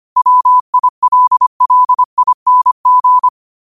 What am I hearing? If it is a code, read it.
Morse code, heard as a keyed tone, about 25 wpm, 1000 Hz. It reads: WILLING
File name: qotdfiles / 25wpm